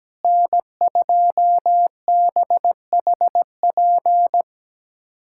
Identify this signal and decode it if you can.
Morse code, heard as a keyed tone, about 17 wpm, 700 Hz. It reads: N2BHP
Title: USA Callsign DE Callsign - 5 Characters - 17wpm